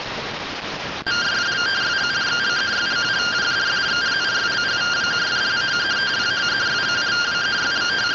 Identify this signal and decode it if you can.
Two-tone teleprinter signal, about 170 Hz shift.